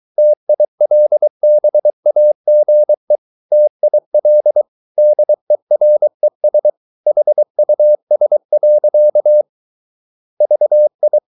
Morse code, heard as a keyed tone, about 23 wpm, 600 Hz. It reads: TILBAGE TIL DERES HUS. VI
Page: Ferie 23 wpm